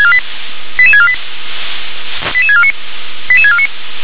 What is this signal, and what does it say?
Five-tone selective call (ZVEI), heard as quick runs of five tones, tones of about 70 ms. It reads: 88648 88649 88649
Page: SDR/ZVEI/88149